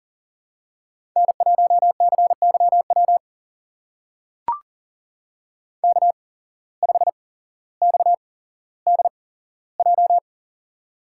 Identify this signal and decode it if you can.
Morse code, heard as a keyed tone, about 39 wpm, spaced as for 17 wpm, 700 Hz. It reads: N1CYW K5XDJ
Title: USA Call Signs - 5 Characters - 8x Character Spacing - 40wpm